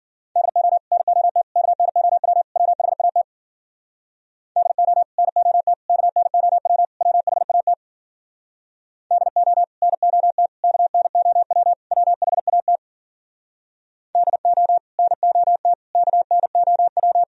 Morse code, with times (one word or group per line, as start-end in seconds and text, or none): 0.4-0.8 DO
0.9-1.4 NOT
1.6-2.4 KNOW
2.6-3.2 WHAT
4.6-5.0 DO
5.2-5.7 NOT
5.9-6.8 KNOW
7.0-7.7 WHAT
9.1-9.6 DO
9.8-10.5 NOT
10.6-11.7 KNOW
11.9-12.8 WHAT
14.2-14.8 DO
15.0-15.7 NOT
15.9-17.2 KNOW